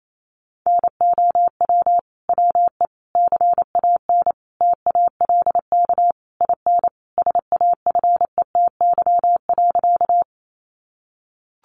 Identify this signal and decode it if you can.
Morse code, heard as a keyed tone, about 28 wpm, 700 Hz. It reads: NOW WE CAN TALK IN SAFETY.
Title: Sentences from Top 700 Words 28wpm